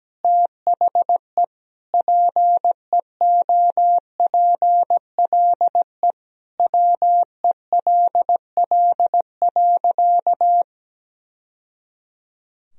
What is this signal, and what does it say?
Morse code, heard as a keyed tone, about 17 wpm, 700 Hz. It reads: THE PEOPLE WELL.
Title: Sentences from Top 1000 Words - Review Part 2 - 17wpm